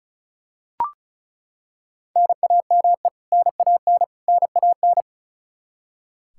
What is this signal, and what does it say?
Morse code, heard as a keyed tone, about 35 wpm, 700 Hz. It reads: NAME NAN NAN
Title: K1USN SST QSO Practice 35wpm